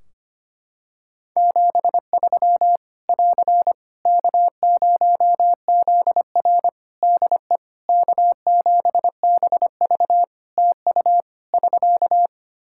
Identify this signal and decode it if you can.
Morse code, heard as a keyed tone, about 25 wpm, 700 Hz. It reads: 73 <AR> K0ZR DE K7BV TU <SK>
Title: K1USN SST QSO Practice 25wpm